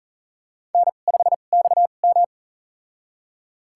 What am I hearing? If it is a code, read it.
Morse code, heard as a keyed tone, about 40 wpm, spaced as for 29 wpm, 700 Hz. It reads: N5XM
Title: USA Call Signs - 4 Characters - 2x Character Spacing - 40wpm